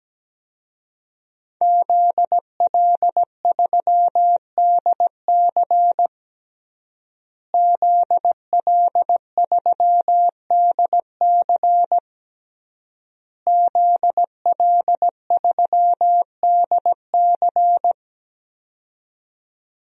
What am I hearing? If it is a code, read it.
Morse code, heard as a keyed tone, about 17 wpm, 700 Hz. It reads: ZL3DC ZL3DC ZL3DC
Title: DXCC-100 Call Signs - 5 Characters - Repeated 3x - 17wpm